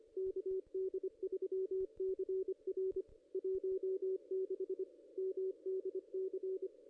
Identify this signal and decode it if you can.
Morse code, heard as a keyed tone, about 25 wpm, 380 Hz, with radio noise. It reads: KD3CR 16 MDC
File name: contests / ss-cw-k5tr-op-n5rz -1900